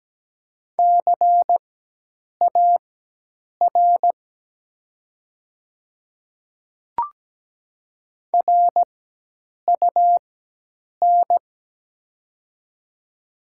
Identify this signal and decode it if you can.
Morse code, heard as a keyed tone, about 17 wpm, spaced as for 8 wpm, 700 Hz. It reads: CAR RUN
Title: 4x Character Spacing - Top 300 Words - 17wpm